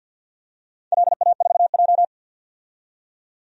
Morse code, heard as a keyed tone, about 48 wpm, 700 Hz. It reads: LA4J
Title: DXCC-100 Call Signs - 4 Characters - 50wpm